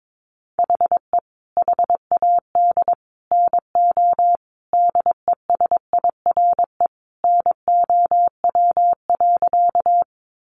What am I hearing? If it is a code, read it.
Morse code, heard as a keyed tone, about 22 wpm, 700 Hz. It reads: HE HAD NO DESIRE NOW.
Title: Sentences from Top 1100 Words 22wpm